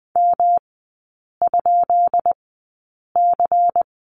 Morse code, heard as a keyed tone, about 19 wpm, spaced as for 8 wpm, 700 Hz. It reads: M?C
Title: 133 Lesson 22e - Question Mark - Three Characters 20wpm